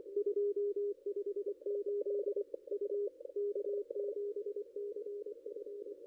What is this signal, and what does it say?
Morse code, heard as a keyed tone, about 22 wpm, 440 Hz. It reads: 258 U K7CF
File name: contests / ss-cw-k5tr-op-n5rz -0600